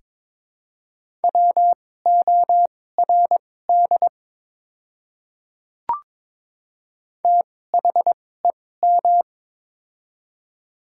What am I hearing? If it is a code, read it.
Morse code, heard as a keyed tone, about 22 wpm, spaced as for 16 wpm, 700 Hz. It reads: WORD THEM
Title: 2x Character Spacing - Top 100 Words - 22wpm